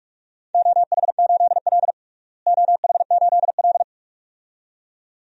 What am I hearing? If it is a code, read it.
Morse code, heard as a keyed tone, about 43 wpm, 700 Hz. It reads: OH8L OH8L
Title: DXCC-100 Call Signs - 4 Characters - Repeated 2x - 45wpm